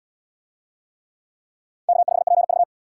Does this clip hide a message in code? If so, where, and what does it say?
Yes: Morse code from 1.9 to 2.6 s, heard as a keyed tone, about 69 wpm, 700 Hz. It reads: K5PV